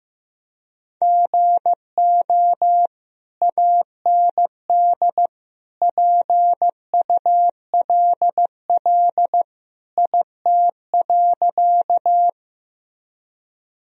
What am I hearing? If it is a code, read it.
Morse code, heard as a keyed tone, about 15 wpm, 700 Hz. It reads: GO AND PULL IT.
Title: Sentences from Top 1000 Words - Review Part 2 - 15wpm